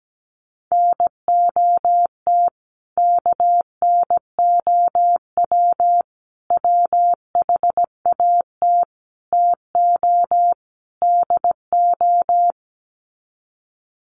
Morse code, heard as a keyed tone, about 17 wpm, 700 Hz. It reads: NOT KNOW WHAT TO DO
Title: Sets of 5 Words 17wpm